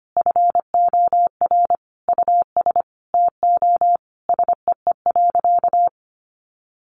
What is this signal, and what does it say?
Morse code, heard as a keyed tone, about 25 wpm, 700 Hz. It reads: FOR US TO SEE.
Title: Sentences from Top 500 Words 25wpm